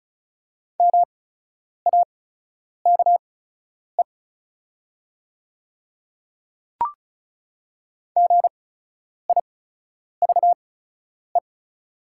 Morse code, heard as a keyed tone, about 36 wpm, spaced as for 10 wpm, 690 Hz. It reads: MAKE GIVE
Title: 8x Character Spacing - Top 200 Words - 35wpm